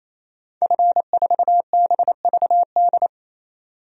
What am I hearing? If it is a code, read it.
Morse code, heard as a keyed tone, about 28 wpm, 700 Hz. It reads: F4BVD
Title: DXCC-100 Call Signs - 5 Characters - 28wpm